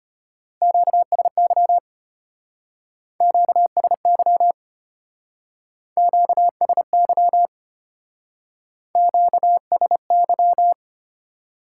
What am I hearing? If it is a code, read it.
Morse code, heard as a keyed tone, about 31 wpm, 700 Hz. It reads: QSY QSY QSY QSY
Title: Speed-Racing Top 90 QSO Elements 25wpm